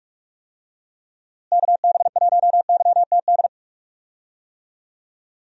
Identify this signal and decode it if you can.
Morse code, heard as a keyed tone, about 43 wpm, 700 Hz. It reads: KB1YTD
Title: USA Call Signs - 6 Characters - Repeated 2x - 45wpm